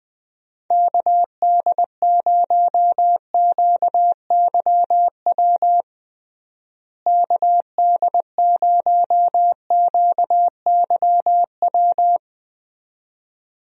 Morse code, heard as a keyed tone, about 20 wpm, 700 Hz. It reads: KD0QYW KD0QYW
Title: USA Call Signs - 6 Characters - Repeated 2x - 20wpm